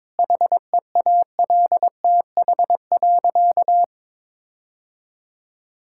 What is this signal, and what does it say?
Morse code, heard as a keyed tone, about 22 wpm, 700 Hz. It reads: HEALTH.
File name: Difficult Sentences with Hints - 22wpm